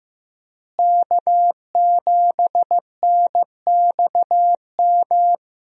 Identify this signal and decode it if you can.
Morse code, heard as a keyed tone, about 15 wpm, 700 Hz. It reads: K7NXM
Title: USA Call Signs - 5 Characters - Repeated 2x - 15wpm